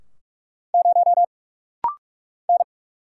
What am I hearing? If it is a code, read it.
Morse code, heard as a keyed tone, about 46 wpm, 700 Hz. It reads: 0 N
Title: Single Letter-Number Alt - Rapid-Fire 45wpm